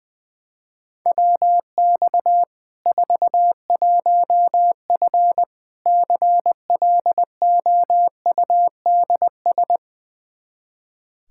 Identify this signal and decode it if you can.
Morse code, heard as a keyed tone, about 21 wpm, 700 Hz. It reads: WX 41F CLOUDS
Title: Practice QSO - Exchange 2 - Info - 20wpm